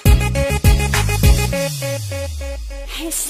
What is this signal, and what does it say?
There is no code.